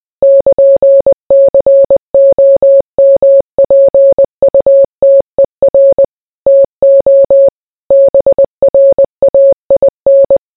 Morse code, heard as a keyed tone, about 20 wpm, 550 Hz. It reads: <KN>COMPUTER TO BRAIN